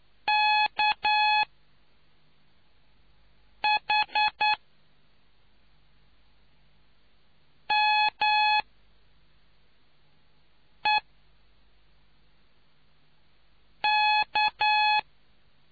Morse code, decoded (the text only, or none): KHMEK